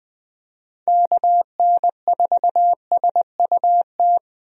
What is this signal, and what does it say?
Morse code, heard as a keyed tone, about 20 wpm, 700 Hz. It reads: KN4SUT
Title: USA Call Signs - 6 Characters - Repeated 2x - 20wpm